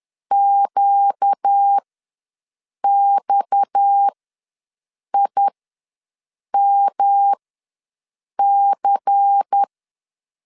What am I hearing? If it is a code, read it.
Morse code, heard as a keyed tone, about 10 wpm, spaced as for 6 wpm, 800 Hz. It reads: QXIMC
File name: LEZIONI / 6 PKX0 / 6 PKX ws 10wpm 3706